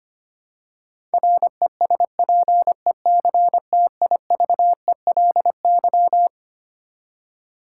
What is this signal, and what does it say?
Morse code, heard as a keyed tone, about 25 wpm, 700 Hz. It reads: RESPECTIVELY
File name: Twelve Letter Words - 25wpm